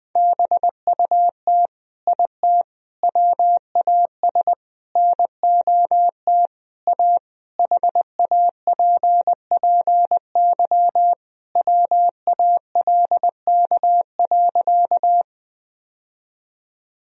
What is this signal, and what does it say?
Morse code, heard as a keyed tone, about 20 wpm, 700 Hz. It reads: BUT IT WAS NOT A HAPPY WALK.
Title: Sentences from Top 1000 Words - Review Part 1 - 20wpm